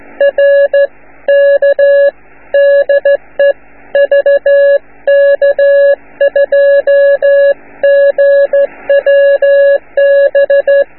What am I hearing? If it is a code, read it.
Morse code, heard as a keyed tone, about 14 wpm, 560 Hz. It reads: R K DE VK2GWB